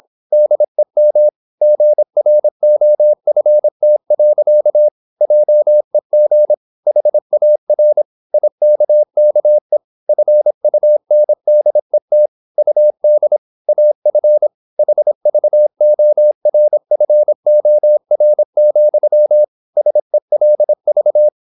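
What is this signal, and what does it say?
Morse code, heard as a keyed tone, about 26 wpm, 600 Hz. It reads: DEM GROFT. JEG HAR IKKE FUNDET UD AF HVORFOR, SELV
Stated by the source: High-speed (QRQ) Morse at 26 wpm.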